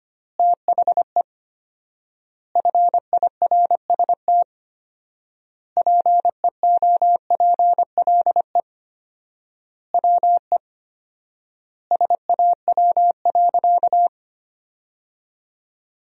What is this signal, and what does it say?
Morse code, heard as a keyed tone, about 25 wpm, 700 Hz. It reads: THE FIRST PEOPLE WE SAW.
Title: 4x Word-Spacing - Sentences from Top 300 Words - 25wpm